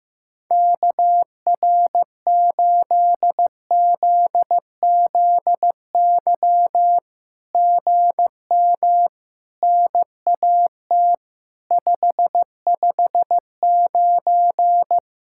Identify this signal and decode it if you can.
Morse code, heard as a keyed tone, about 15 wpm, 700 Hz. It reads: KR8ZZY GM NAT 559
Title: SOTA QSO Practice 15wpm